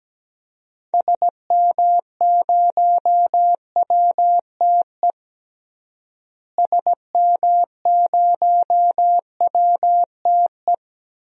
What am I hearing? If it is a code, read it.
Morse code, heard as a keyed tone, about 17 wpm, 700 Hz. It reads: SM0WTE SM0WTE
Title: DXCC-100 Call Signs - 6 Characters - Repeated 2x - 17wpm